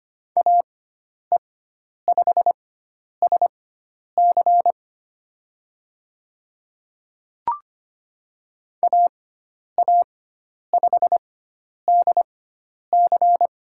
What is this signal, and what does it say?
Morse code, heard as a keyed tone, about 25 wpm, spaced as for 10 wpm, 700 Hz. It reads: AE5SC AA5DC
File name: Lesson 16g - Letter c - Call Signs 25wpm